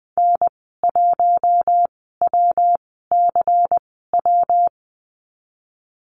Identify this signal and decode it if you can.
Morse code, heard as a keyed tone, about 20 wpm, spaced as for 14 wpm, 700 Hz. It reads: N1WCW
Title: 128 Lesson 21g - Number 6 - Call Signs 20wpm